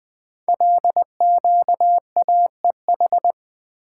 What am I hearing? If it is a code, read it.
Morse code, heard as a keyed tone, about 20 wpm, 700 Hz. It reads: LQAEH
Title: Code Groups - 5 Letters - 20wpm